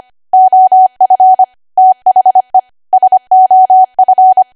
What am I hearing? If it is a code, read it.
Morse code, heard as a keyed tone, about 25 wpm, 750 Hz. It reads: OF THE SOF